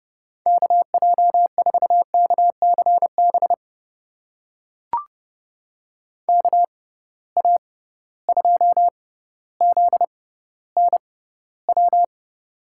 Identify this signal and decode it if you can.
Morse code, heard as a keyed tone, about 32 wpm, spaced as for 16 wpm, 700 Hz. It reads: KJ4KCB KA2ZNW
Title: USA Call Signs - 6 Characters - 6x Character Spacing - 30wpm